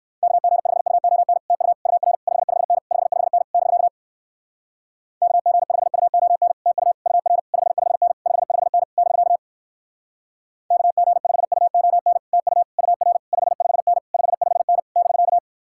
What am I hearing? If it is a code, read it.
Morse code, heard as a keyed tone, about 59 wpm, 700 Hz. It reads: KG5UON TU UR 55N 55N <BK> KG5UON TU UR 55N 55N <BK> KG5UON TU UR 55N 55N <BK>